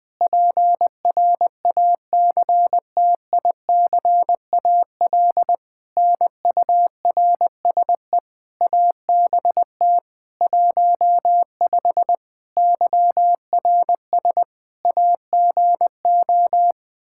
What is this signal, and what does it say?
Morse code, heard as a keyed tone, about 21 wpm, 700 Hz. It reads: PRACTICAL NURSE ABT 15 YRS AGO
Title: K1USN SST QSO Practice 20wpm